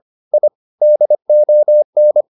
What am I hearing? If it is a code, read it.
Morse code, heard as a keyed tone, about 25 wpm, 600 Hz. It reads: I DON
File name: the old man and the sea part 1 25wpm